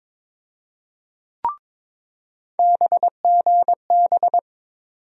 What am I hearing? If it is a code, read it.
Morse code, heard as a keyed tone, about 21 wpm, 700 Hz. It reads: BGB
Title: USA Call Sign Postfixes - 3 Letters 22wpm